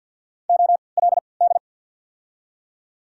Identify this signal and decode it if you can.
Morse code, heard as a keyed tone, about 53 wpm, spaced as for 29 wpm, 700 Hz. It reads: OLD
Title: 3x Character Spacing - Top 200 Words - 50wpm